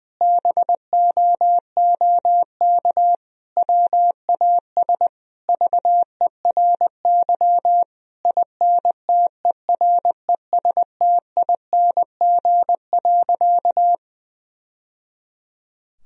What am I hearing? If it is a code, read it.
Morse code, heard as a keyed tone, about 21 wpm, 700 Hz. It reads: BOOK WAS VERY INTERESTING.
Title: Sentences from Top 1000 Words - Review Part 4 - 20wpm